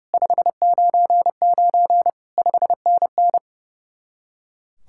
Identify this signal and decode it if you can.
Morse code, heard as a keyed tone, about 38 wpm, 700 Hz. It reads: 599 5NN